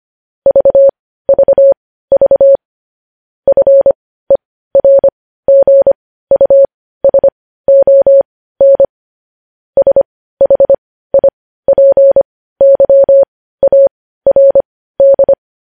Morse code, heard as a keyed tone, about 25 wpm, spaced as for 15 wpm, 550 Hz. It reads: VVV FERGUSON SHIPYARD
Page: news headlines in morse code at 15 wpm